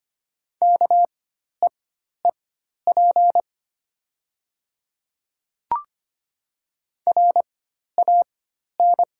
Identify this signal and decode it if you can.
Morse code, heard as a keyed tone, about 25 wpm, spaced as for 12 wpm, 690 Hz. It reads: KEEP RAN
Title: 4x Character Spacing - Top 500 Words - 25wpm